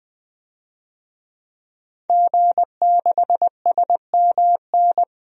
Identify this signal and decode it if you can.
Morse code, heard as a keyed tone, about 21 wpm, 700 Hz. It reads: G6SMN
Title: DXCC-100 Call Signs - 5 Characters - 20wpm